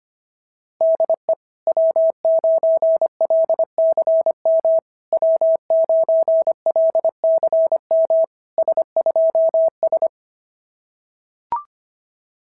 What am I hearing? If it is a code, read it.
Morse code, heard as a keyed tone, about 26 wpm, 690 Hz. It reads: DE W9LCM W9LCM S2S
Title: SOTA QSO Practice 25wpm